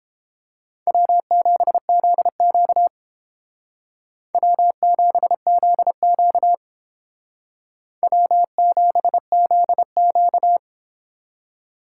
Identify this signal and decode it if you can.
Morse code, heard as a keyed tone, about 30 wpm, 700 Hz. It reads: W7ZQ W7ZQ W7ZQ